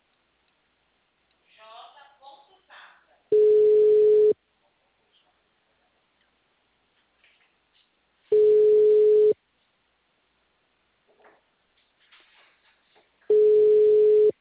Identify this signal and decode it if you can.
Ringback tone (Europe, 425 Hz), with speech.
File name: out- -2039- - - .29366